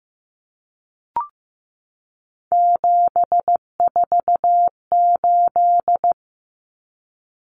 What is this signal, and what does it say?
Morse code, heard as a keyed tone, about 15 wpm, 700 Hz. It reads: 748
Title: Common Cut Numbers - 3-Digit 15wpm